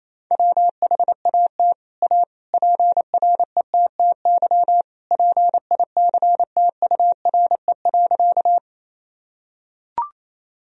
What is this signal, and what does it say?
Morse code, heard as a keyed tone, about 27 wpm, 700 Hz. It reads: WHAT A PRETTY PICTURE.
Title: Sentences from Top 1000 Words - Review Part 1 - 28wpm